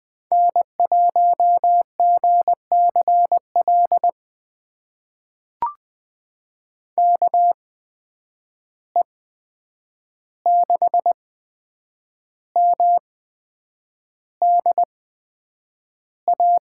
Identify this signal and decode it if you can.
Morse code, heard as a keyed tone, about 20 wpm, spaced as for 8 wpm, 700 Hz. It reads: N1GCL KE6MDA
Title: USA Call Signs - 8x Character Spacing - 20wpm